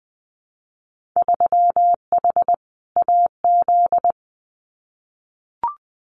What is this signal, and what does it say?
Morse code, heard as a keyed tone, about 20 wpm, 700 Hz. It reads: 3H AZ
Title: Winter Field Day Exchanges Encore 20wpm